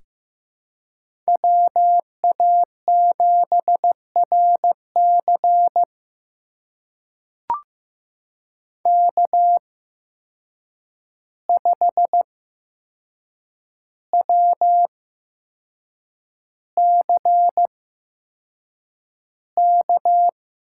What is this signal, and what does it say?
Morse code, heard as a keyed tone, about 15 wpm, spaced as for 6 wpm, 700 Hz. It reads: WA7RC K5WCK
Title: USA Call Signs - 5 Characters - 8x Character Spacing - 15wpm